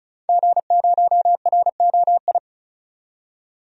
Morse code, heard as a keyed tone, about 36 wpm, 700 Hz. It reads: G0ROI
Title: DXCC-100 Call Signs - 5 Characters - 35wpm